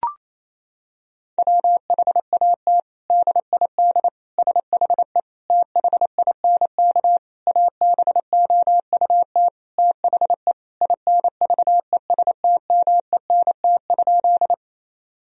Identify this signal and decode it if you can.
Morse code, heard as a keyed tone, about 28 wpm, 700 Hz. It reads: WHAT DID SHE THINK ABOUT THE INVESTMENT?
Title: Sentences from Top 1000 Words - Review Part 4 - 28wpm